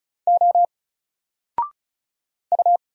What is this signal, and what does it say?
Morse code, heard as a keyed tone, about 34 wpm, 700 Hz. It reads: O U
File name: Single-Letter-Number-35wpm